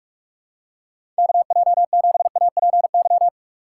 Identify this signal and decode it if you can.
Morse code, heard as a keyed tone, about 46 wpm, 700 Hz. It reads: KJ7APY
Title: USA Call Signs - 6 Characters - 6x Character Spacing - 45wpm